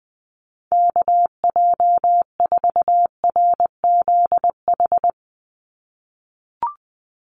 Morse code, heard as a keyed tone, about 20 wpm, 700 Hz. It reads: KJ4RZH
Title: USA Call Signs 20wpm